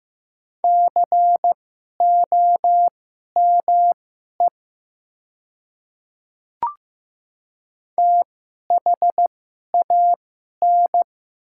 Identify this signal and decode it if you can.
Morse code, heard as a keyed tone, about 15 wpm, spaced as for 11 wpm, 700 Hz. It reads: COME THAN
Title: 2x Character Spacing - Top 300 Words - 15wpm